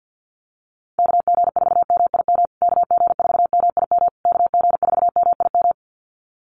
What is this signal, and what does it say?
Morse code, heard as a keyed tone, about 50 wpm, 700 Hz. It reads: KG4MIM KG4MIM KG4MIM
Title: USA Call Signs Repeated 3x 50wpm